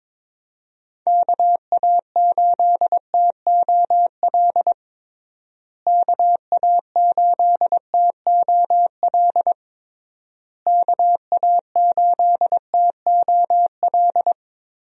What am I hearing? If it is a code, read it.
Morse code, heard as a keyed tone, about 22 wpm, 700 Hz. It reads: KA8TOL KA8TOL KA8TOL
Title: USA Call Signs - 6 Characters - Repeated 3x - 22wpm